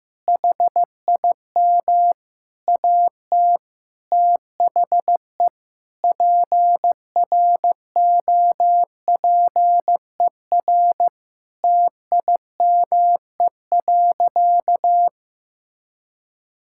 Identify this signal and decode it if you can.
Morse code, heard as a keyed tone, about 15 wpm, 700 Hz. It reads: HIM AT THE PROPER TIME.
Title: Sentences from Top 1000 Words - Review Part 5 - 15wpm